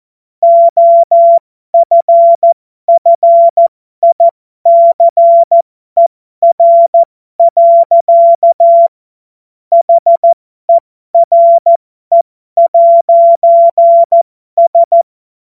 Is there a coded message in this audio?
Yes: Morse code from 0.4 to 15.0 s, heard as a keyed tone, about 14 wpm, spaced as for 12 wpm, 680 Hz. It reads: OFFICER. HERE'S